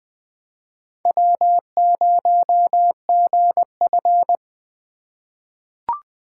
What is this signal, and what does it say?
Morse code, heard as a keyed tone, about 20 wpm, 700 Hz. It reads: W0GF
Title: USA Call Signs - 4 Characters - 3x Character Spacing - 20wpm